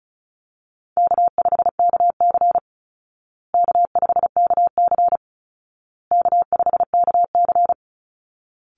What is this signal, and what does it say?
Morse code, heard as a keyed tone, about 34 wpm, 700 Hz. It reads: K5KC K5KC K5KC
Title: Easy Call Signs Repeated 3x - 35wpm